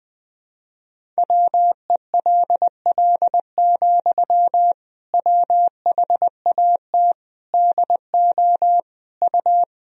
Morse code, heard as a keyed tone, about 20 wpm, 700 Hz. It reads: WELL, WHAT DO U
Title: Sentences from Top 200 Words and Abbreviations 20wpm